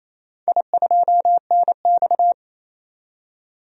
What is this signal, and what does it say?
Morse code, heard as a keyed tone, about 28 wpm, 700 Hz. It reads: I2NX
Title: DXCC-100 Call Signs - 4 Characters - 28wpm